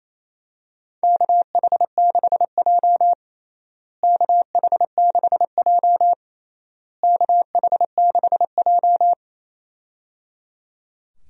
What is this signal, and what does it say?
Morse code, heard as a keyed tone, about 28 wpm, 700 Hz. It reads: KH6J KH6J KH6J
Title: USA Call Signs - 4 Characters - Repeated 3x - 28wpm